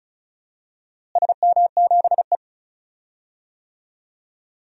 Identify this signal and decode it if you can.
Morse code, heard as a keyed tone, about 34 wpm, 700 Hz. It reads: SM7E
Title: DXCC-100 Call Signs - 4 Characters - Repeated 3x - 35wpm